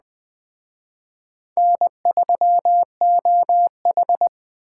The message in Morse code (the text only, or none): N3OH